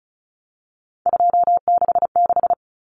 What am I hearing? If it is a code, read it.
Morse code, heard as a keyed tone, about 34 wpm, 700 Hz. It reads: 266
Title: Common Cut Numbers - 3-Digit 35wpm